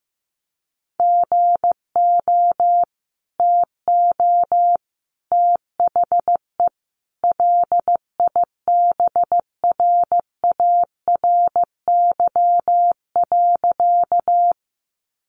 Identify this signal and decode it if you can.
Morse code, heard as a keyed tone, about 15 wpm, 700 Hz. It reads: GO TO THE LIBRARY.
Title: Sentences from Top 600 Words 15wpm